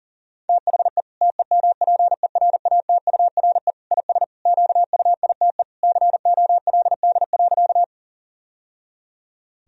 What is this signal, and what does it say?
Morse code, heard as a keyed tone, about 40 wpm, 700 Hz. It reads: THE TEMPERATURE IS QUITE COLD.
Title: Sentences from Top 1000 Words - Review Part 4 - 40wpm